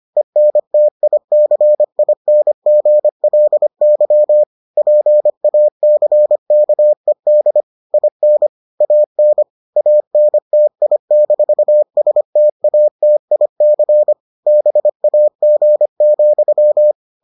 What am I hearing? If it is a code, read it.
Morse code, heard as a keyed tone, about 25 wpm, 600 Hz. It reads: ENTICINGLY PACKED IN AN ANTI-STATIC BAG,